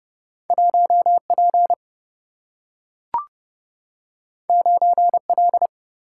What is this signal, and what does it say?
Morse code, heard as a keyed tone, about 30 wpm, 700 Hz. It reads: 1 P 9 L
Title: Single-Letter Single-Number 30wpm